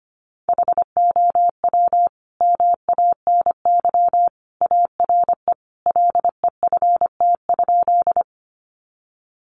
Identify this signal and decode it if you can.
Morse code, heard as a keyed tone, about 25 wpm, 700 Hz. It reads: HOW MANY ARE LEFT?
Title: Sentences from Top 300 Words 25wpm